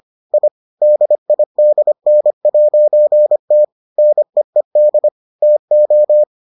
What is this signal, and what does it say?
Morse code, heard as a keyed tone, about 25 wpm, 600 Hz. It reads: I DIDN'T NEED TO